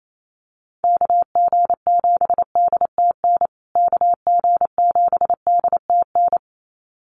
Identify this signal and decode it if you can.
Morse code, heard as a keyed tone, about 28 wpm, 700 Hz. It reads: KG7DTN KG7DTN
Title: USA Call Signs Repeated 2x 28wpm